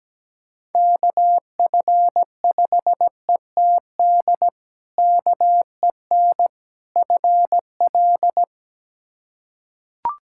Morse code, heard as a keyed tone, about 17 wpm, 700 Hz. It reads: KF5ETD KEN FL
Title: K1USN SST QSO Practice - 17wpm